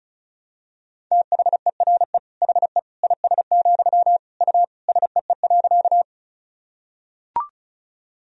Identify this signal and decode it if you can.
Morse code, heard as a keyed tone, about 36 wpm, 700 Hz. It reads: THERE HE IS, U SEE.
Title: Sentences from Top 100 Words and Abbreviations 35wpm